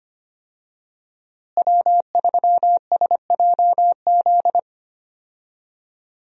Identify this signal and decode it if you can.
Morse code, heard as a keyed tone, about 25 wpm, 700 Hz. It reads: W3SJZ
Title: USA Call Signs - 5 Characters - Repeated 2x - 25wpm